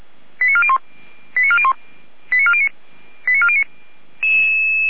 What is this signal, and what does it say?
Five-tone selective call (ZVEI), heard as quick runs of five tones, tones of about 70 ms. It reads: 89401 89401 89409 89409